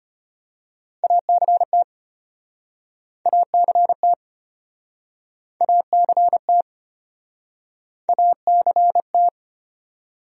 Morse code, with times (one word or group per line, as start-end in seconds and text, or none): 1.0-1.8 ACT
3.3-4.1 ACT
5.6-6.6 ACT
8.1-9.3 ACT